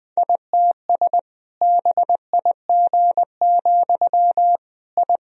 Morse code, heard as a keyed tone, about 21 wpm, 700 Hz. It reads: ITS BIG, I
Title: Sentences from Top 1000 Words - Review Part 1 - 20wpm